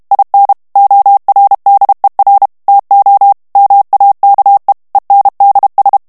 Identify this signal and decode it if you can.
Morse code, heard as a keyed tone, about 32 wpm, 800 Hz. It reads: IN ORDER TO MAKE ENDS
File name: morse mp3 / ANGLAIS texte clair 1 vitesse 30 mots minute